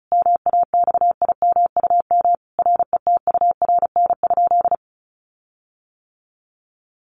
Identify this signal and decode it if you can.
Morse code, heard as a keyed tone, about 34 wpm, 700 Hz. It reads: MAXIMUM RETURN?
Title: Sentences from Top 1300 Words 35wpm